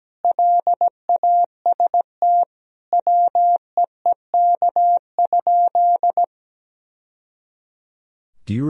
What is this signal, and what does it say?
Morse code, heard as a keyed tone, about 17 wpm, 700 Hz. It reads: LAST WEEK?
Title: Sentences from Top 1000 Words - Review Part 3 - 17wpm